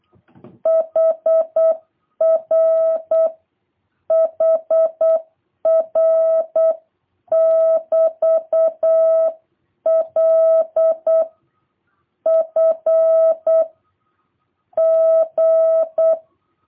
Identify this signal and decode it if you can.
Morse code, heard as a keyed tone, about 8 wpm, 660 Hz. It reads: HR HR <BT> L F G